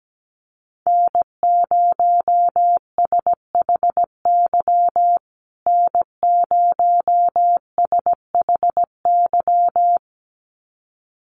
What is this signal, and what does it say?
Morse code, heard as a keyed tone, about 17 wpm, 700 Hz. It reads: N0SHY N0SHY
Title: USA Call Signs Repeated 2x 17wpm